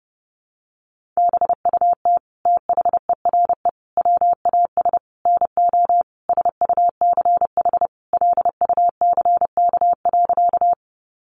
Morse code, heard as a keyed tone, about 30 wpm, 700 Hz. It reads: BUT THERE WAS NO SUCH LUCK.
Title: Sentences from Top 1500 Words 30wpm